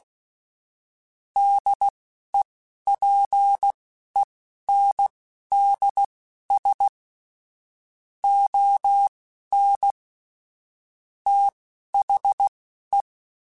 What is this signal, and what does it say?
Morse code, heard as a keyed tone, about 16 wpm, spaced as for 11 wpm, 800 Hz. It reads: DEPENDS ON THE
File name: Technical Subjects / Technical Subjects-mp3s / Sw Pwr Sup @10